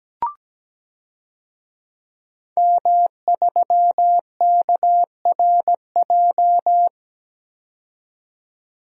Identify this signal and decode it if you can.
Morse code, heard as a keyed tone, about 17 wpm, 700 Hz. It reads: M3KRJ
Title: DXCC-100 Call Signs - 5 Characters - 17wpm